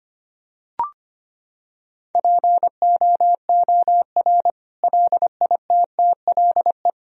Morse code, heard as a keyed tone, about 25 wpm, 700 Hz. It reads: POOR LITTLE
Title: Sentences from Top 1000 Words - Review Part 3 - 25wpm